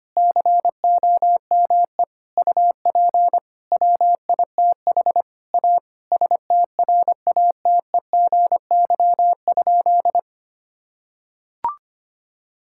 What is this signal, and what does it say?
Morse code, heard as a keyed tone, about 25 wpm, 700 Hz. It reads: COME UP WITH A STRATEGY?
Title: Sentences from Top 1000 Words - Review Part 5 - 25wpm